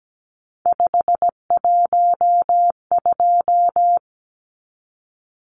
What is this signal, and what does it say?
Morse code, heard as a keyed tone, about 17 wpm, 700 Hz. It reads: 512
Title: Numbers - 3-Digit 17wpm